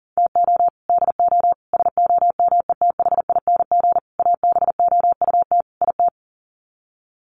Morse code, heard as a keyed tone, about 40 wpm, 700 Hz. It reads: TO DO SOMETHING ABOUT IT
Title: Sets of 5 Words 40wpm